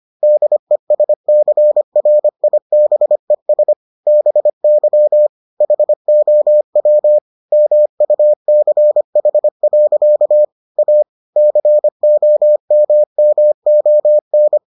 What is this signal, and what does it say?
Morse code, heard as a keyed tone, about 25 wpm, 600 Hz. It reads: DESCRIBES BY HOW MUCH. A COMMON